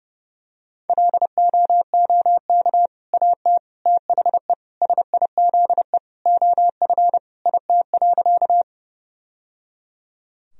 Morse code, heard as a keyed tone, about 30 wpm, 700 Hz. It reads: LOOK AT THE SIZE OF IT.
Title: Sentences from Top 1000 Words - Review Part 2 - 30wpm